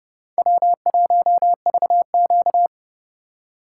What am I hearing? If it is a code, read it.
Morse code, heard as a keyed tone, about 34 wpm, 700 Hz. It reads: W1VQ
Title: Speed Racing - USA 1x2 Call Signs 30wpm